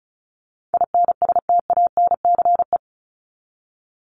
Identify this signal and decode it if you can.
Morse code, heard as a keyed tone, about 34 wpm, 700 Hz. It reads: INSTANCE
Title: Top 501-750 Words 35wpm